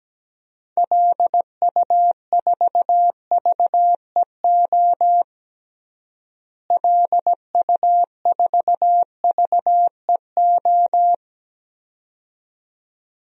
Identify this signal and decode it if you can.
Morse code, heard as a keyed tone, about 17 wpm, 700 Hz. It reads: LU4VEO LU4VEO
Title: DXCC-100 Call Signs - 6 Characters - Repeated 2x - 17wpm